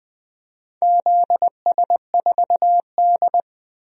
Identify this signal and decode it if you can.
Morse code, heard as a keyed tone, about 20 wpm, 700 Hz. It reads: ZS4D
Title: DXCC-100 Call Signs - 4 Characters - 20wpm